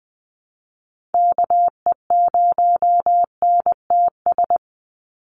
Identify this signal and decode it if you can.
Morse code, heard as a keyed tone, about 20 wpm, 700 Hz. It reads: KE0NTS
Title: USA Call Signs 20wpm